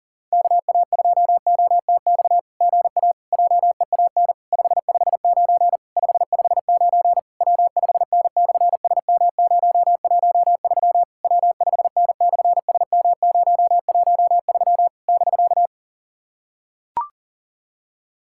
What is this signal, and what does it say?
Morse code, heard as a keyed tone, about 43 wpm, 700 Hz. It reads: KA2OTX GA JEAN 559 559 W5N/SM013 W5N/SM013 <BK>
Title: SOTA QSO Practice 40wpm